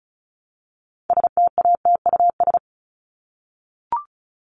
Morse code, heard as a keyed tone, about 34 wpm, 700 Hz. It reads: STATUS
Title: Top 1251-1500 Words 35wpm